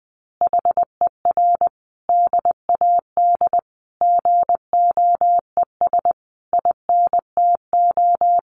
Morse code, heard as a keyed tone, about 20 wpm, 700 Hz. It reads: HER DAD GOES INTO
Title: Additional Easy Phrases 20wpm